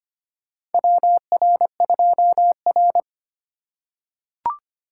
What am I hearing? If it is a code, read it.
Morse code, heard as a keyed tone, about 25 wpm, 700 Hz. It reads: WR2R
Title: USA Call Signs - 4 Characters - 2x Character Spacing - 25wpm